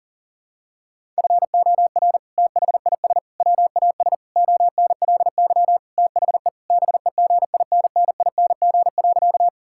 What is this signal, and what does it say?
Morse code, heard as a keyed tone, about 37 wpm, 690 Hz. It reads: FOR THIS WAS ONLY THE BEGINNING.
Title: Sentences from Top 1000 Words - Review Part 5 - 40wpm